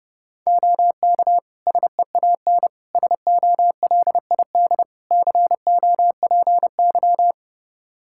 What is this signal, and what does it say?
Morse code, heard as a keyed tone, about 30 wpm, 700 Hz. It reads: OK SEAN SOLID COPY
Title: Practice QSO - Exchange 3 - More Info - 30wpm